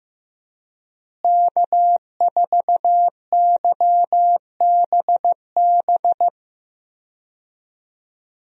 Morse code, heard as a keyed tone, about 15 wpm, 700 Hz. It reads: K4YBB
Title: USA Call Signs - 5 Characters - Repeated 2x - 15wpm